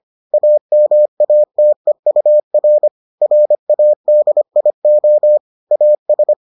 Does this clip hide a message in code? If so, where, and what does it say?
Yes: Morse code from 0.3 to 6.3 s, heard as a keyed tone, about 25 wpm, 600 Hz. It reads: AMATEUR RADIO AS